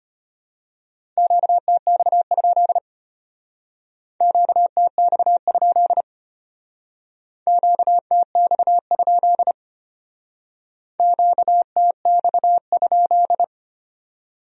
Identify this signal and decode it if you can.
Morse code, heard as a keyed tone, about 32 wpm, 700 Hz. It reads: QTX? QTX? QTX? QTX?